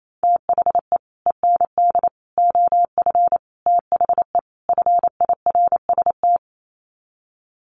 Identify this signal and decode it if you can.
Morse code, heard as a keyed tone, about 28 wpm, 700 Hz. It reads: THE END OF THE FIRST
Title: Sets of 5 Words Encore 28wpm